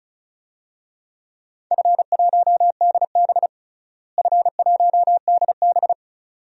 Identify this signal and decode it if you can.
Morse code, heard as a keyed tone, about 34 wpm, 700 Hz. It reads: F1DB F1DB
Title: DXCC-100 Call Signs - 4 Characters - Repeated 2x - 35wpm